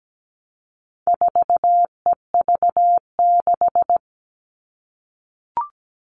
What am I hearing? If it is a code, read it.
Morse code, heard as a keyed tone, about 17 wpm, 700 Hz. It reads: 4EV6
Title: All Cut Numbers - 4-Digit 17wpm